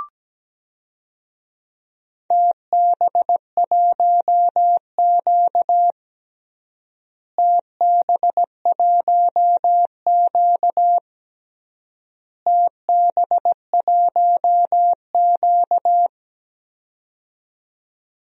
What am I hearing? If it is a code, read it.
Morse code, heard as a keyed tone, about 17 wpm, 700 Hz. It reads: TB1Q TB1Q TB1Q
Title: DXCC-100 Call Signs - 4 Characters - Repeated 3x - 17wpm